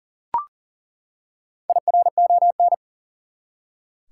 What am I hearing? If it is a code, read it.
Morse code, heard as a keyed tone, about 40 wpm, 700 Hz. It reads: IRON
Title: Four Letter Words - 40wpm